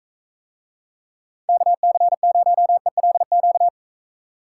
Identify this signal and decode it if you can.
Morse code, heard as a keyed tone, about 45 wpm, 700 Hz. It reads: KC0ELQ